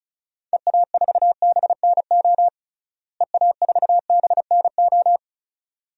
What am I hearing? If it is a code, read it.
Morse code, heard as a keyed tone, about 36 wpm, 700 Hz. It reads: EA4BNO EA4BNO
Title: DXCC-100 Call Signs - 6 Characters - Repeated 2x - 35wpm